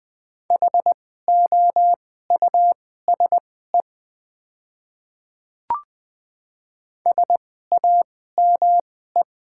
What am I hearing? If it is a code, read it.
Morse code, heard as a keyed tone, about 20 wpm, spaced as for 14 wpm, 700 Hz. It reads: HOUSE SAME
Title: 2x Character Spacing - Top 200 Words - 20wpm